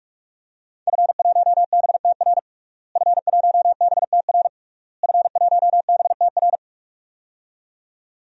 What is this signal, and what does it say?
Morse code, heard as a keyed tone, about 44 wpm, 700 Hz. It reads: F1BTR F1BTR F1BTR
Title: DXCC-100 Call Signs - 5 Characters - Repeated 3x - 45wpm